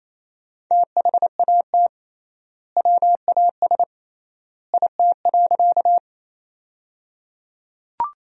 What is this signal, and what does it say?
Morse code, heard as a keyed tone, about 28 wpm, 700 Hz. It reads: THAT WAS IT.
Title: 3x Word-Spacing - Sentences from Top 100 Words - 28wpm